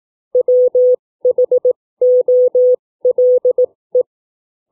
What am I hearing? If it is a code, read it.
Morse code, heard as a keyed tone, about 18 wpm, spaced as for 15 wpm, 500 Hz. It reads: WHOLE
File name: CW / book reading 2